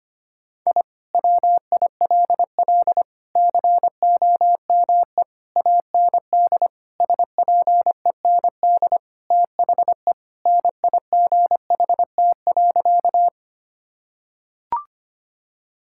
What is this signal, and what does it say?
Morse code, heard as a keyed tone, about 25 wpm, 700 Hz. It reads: I WILL COME AND SPEND THE NIGHT.
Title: Sentences from Top 1000 Words - Review Part 5 - 25wpm